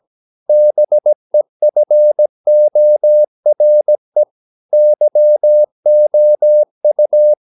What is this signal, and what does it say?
Morse code, heard as a keyed tone, about 17 wpm, 600 Hz. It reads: BEFORE YOU
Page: Romanen Never af Ken Follet på engelsk.